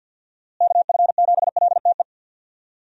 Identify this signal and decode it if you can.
Morse code, heard as a keyed tone, about 52 wpm, 700 Hz. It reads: KF7LTE
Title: USA Call Signs - 6 Characters - Repeated 2x - 50wpm